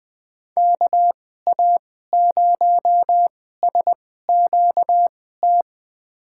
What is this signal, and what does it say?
Morse code, heard as a keyed tone, about 20 wpm, spaced as for 14 wpm, 700 Hz. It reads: KA0SQT
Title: USA Call Signs - 6 Characters - 2x Character Spacing - 20wpm